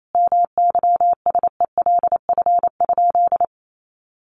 Morse code, heard as a keyed tone, about 28 wpm, 700 Hz. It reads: MYSELF?
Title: Sentences from Top 1300 Words 28wpm